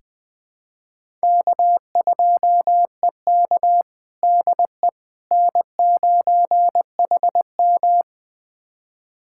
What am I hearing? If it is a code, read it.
Morse code, heard as a keyed tone, about 20 wpm, 700 Hz. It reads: K2EK DE N9HM
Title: USA Callsign DE Callsign - 4 Characters - 20wpm